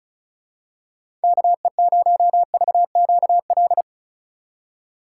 Morse code, heard as a keyed tone, about 34 wpm, 700 Hz. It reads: KE0VQL
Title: USA Call Signs - 6x Character Spacing - 35wpm